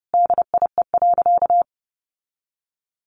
Morse code, heard as a keyed tone, about 28 wpm, 700 Hz. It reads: DIE.
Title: Sentences from Top 1400 Words 30wpm